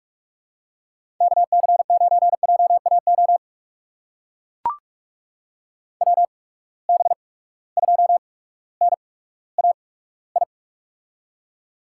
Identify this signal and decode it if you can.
Morse code, heard as a keyed tone, about 44 wpm, spaced as for 19 wpm, 700 Hz. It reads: KC9JAO WB2NAI
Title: USA Call Signs - 6 Characters - 8x Character Spacing - 45wpm